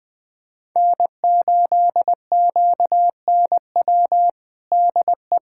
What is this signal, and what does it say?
Morse code, heard as a keyed tone, about 20 wpm, 700 Hz. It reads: N8QNW DE
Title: USA Callsign DE Callsign 20wpm